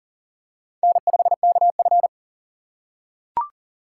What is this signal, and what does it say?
Morse code, heard as a keyed tone, about 40 wpm, 700 Hz. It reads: N5KF
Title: USA Call Signs - 4 Characters - 6x Character Spacing - 40wpm